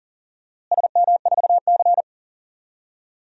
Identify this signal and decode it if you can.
Morse code, heard as a keyed tone, about 39 wpm, 700 Hz. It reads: SM4C
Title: DXCC-100 Call Signs - 4 Characters - Repeated 2x - 40wpm